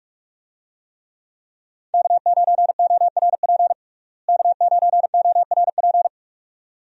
Morse code, heard as a keyed tone, about 44 wpm, 700 Hz. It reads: K9ORP K9ORP
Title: USA Call Signs - 5 Characters - Repeated 2x - 45wpm